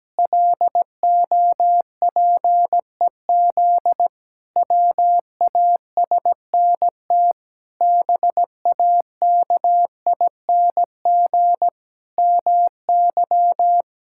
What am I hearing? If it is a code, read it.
Morse code, heard as a keyed tone, about 17 wpm, 700 Hz. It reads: LOPEZ WASNT BAKING MY
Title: Pangram Challenge v2 - 17wpm